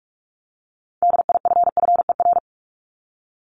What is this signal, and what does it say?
Morse code, heard as a keyed tone, about 44 wpm, 700 Hz. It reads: DIFFER
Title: Top 500 Words 45wpm